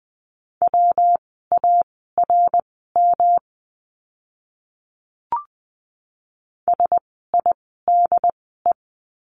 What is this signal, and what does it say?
Morse code, heard as a keyed tone, about 20 wpm, spaced as for 14 wpm, 700 Hz. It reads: WARM SIDE
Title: 115 Lesson 19f - Letter w - Words 20wpm